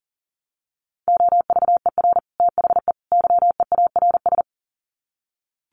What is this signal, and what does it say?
Morse code, heard as a keyed tone, about 40 wpm, 700 Hz. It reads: OVER THE YEARS
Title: Sets of 3 Words 40wpm